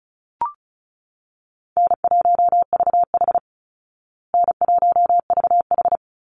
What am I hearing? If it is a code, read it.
Morse code, heard as a keyed tone, about 34 wpm, 700 Hz. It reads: N1VH N1VH
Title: Call Signs Repeated 3x - 35wpm